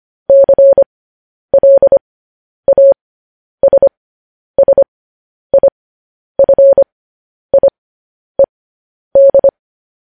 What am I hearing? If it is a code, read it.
Morse code, heard as a keyed tone, about 25 wpm, spaced as for 10 wpm, 550 Hz. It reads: CLASSIFIED
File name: podcast / News headlines-1x-25x10